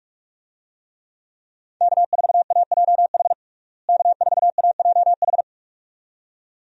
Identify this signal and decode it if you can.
Morse code, heard as a keyed tone, about 45 wpm, 700 Hz. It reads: K4AJH K4AJH
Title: USA Call Signs - 5 Characters - Repeated 2x - 45wpm